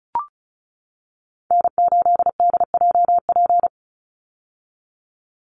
Morse code, heard as a keyed tone, about 34 wpm, 700 Hz. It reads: N8DJP
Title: USA Call Signs 35wpm